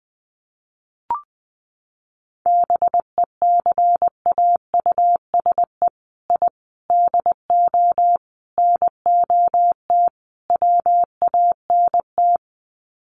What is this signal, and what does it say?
Morse code, heard as a keyed tone, about 20 wpm, 700 Hz. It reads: BECAUSE I DO NOT WANT
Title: Sets of 5 Words Encore 20wpm